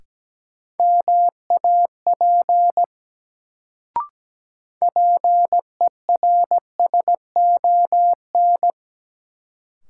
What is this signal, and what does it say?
Morse code, heard as a keyed tone, about 17 wpm, 700 Hz. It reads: MAP PERSON
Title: Sentences from Top 400 Words - New Word Drill - 17wpm